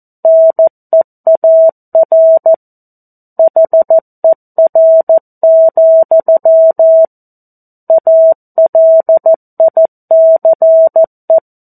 Morse code, heard as a keyed tone, about 14 wpm, 650 Hz. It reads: NEAR HER, ALICE
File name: 02-Chapter-02 14 WPM